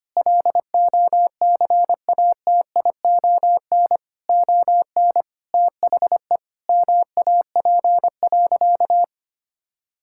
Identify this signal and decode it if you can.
Morse code, heard as a keyed tone, about 25 wpm, 700 Hz. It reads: LOCATION ON THE MAP.
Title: Sentences from Top 1000 Words - Review Part 6 - 25wpm